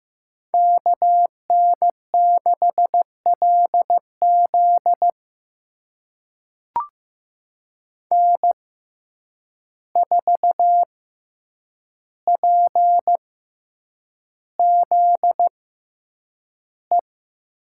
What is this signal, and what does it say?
Morse code, heard as a keyed tone, about 15 wpm, spaced as for 8 wpm, 700 Hz. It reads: KN6LZ N4PZE
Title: USA Call Signs - 5 Characters - 6x Character Spacing - 15wpm